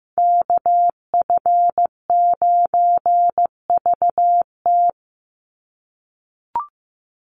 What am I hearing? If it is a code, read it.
Morse code, heard as a keyed tone, about 15 wpm, 700 Hz. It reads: KF9VT
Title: USA Call Signs 15wpm